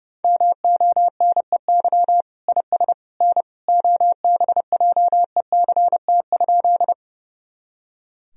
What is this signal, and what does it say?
Morse code, heard as a keyed tone, about 30 wpm, 700 Hz. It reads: MONEY IS N OBJECT?
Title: Sentences from Top 500 Words and Abbreviations 30wpm